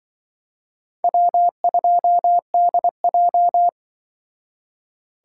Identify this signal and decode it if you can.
Morse code, heard as a keyed tone, about 25 wpm, 700 Hz. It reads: W2DJ
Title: Speed Racing - USA 1x2 Call Signs 20wpm